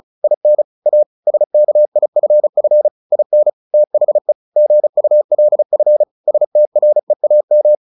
Morse code, heard as a keyed tone, about 34 wpm, 600 Hz. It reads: IN A SKIFF IN THE GULF STREAM
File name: the old man and the sea / the old man and the sea 1 35wpm